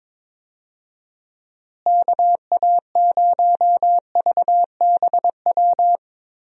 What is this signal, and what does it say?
Morse code, heard as a keyed tone, about 22 wpm, 700 Hz. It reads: KA0VBW
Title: USA Call Signs - 6 Characters - Repeated 2x - 22wpm